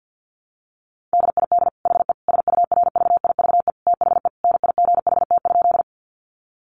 Morse code, heard as a keyed tone, about 50 wpm, 700 Hz. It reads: DID HE SURVIVE THE NIGHT?
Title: Sentences from Top 1300 Words 50wpm